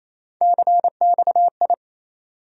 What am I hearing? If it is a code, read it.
Morse code, heard as a keyed tone, about 28 wpm, 700 Hz. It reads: CXI
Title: USA Call Sign Postfixes - 3 Letters 28wpm